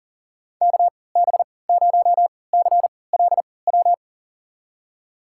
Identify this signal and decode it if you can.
Morse code, heard as a keyed tone, about 37 wpm, spaced as for 22 wpm, 700 Hz. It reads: KB0CLW
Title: USA Call Signs - 3x Character Spacing - 40wpm